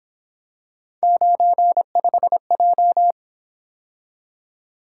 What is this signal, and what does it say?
Morse code, heard as a keyed tone, about 26 wpm, 700 Hz. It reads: 95J
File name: Speed-Racing - Three Letter-Number Combinations - 22wpm